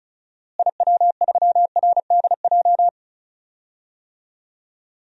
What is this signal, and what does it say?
Morse code, heard as a keyed tone, about 34 wpm, 700 Hz. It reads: IW3RDJ
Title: DXCC-100 Call Signs - 6 Characters - Repeated 2x - 35wpm